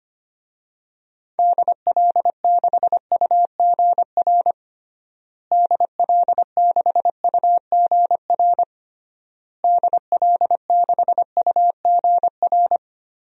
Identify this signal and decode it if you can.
Morse code, heard as a keyed tone, about 25 wpm, 690 Hz. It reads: DL6UGR DL6UGR DL6UGR
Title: DXCC-100 Call Signs - 6 Characters - Repeated 3x - 25wpm